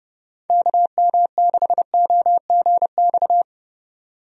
Morse code, heard as a keyed tone, about 30 wpm, 700 Hz. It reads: KM6OGX
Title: USA Call Signs - 6 Characters - 30wpm